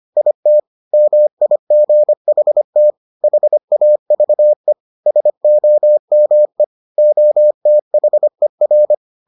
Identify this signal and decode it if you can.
Morse code, heard as a keyed tone, about 25 wpm, 600 Hz. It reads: IT MIGHT HAVE SOME OTHER